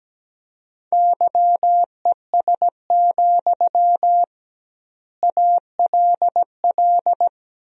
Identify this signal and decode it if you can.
Morse code, heard as a keyed tone, about 17 wpm, 700 Hz. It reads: YES, ALL
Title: 2x Word-Spacing - Sentences from Top 500 Words - 17wpm